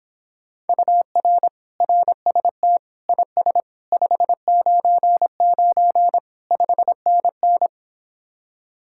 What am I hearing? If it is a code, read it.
Morse code, heard as a keyed tone, about 26 wpm, 700 Hz. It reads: UR RST IS 599 5NN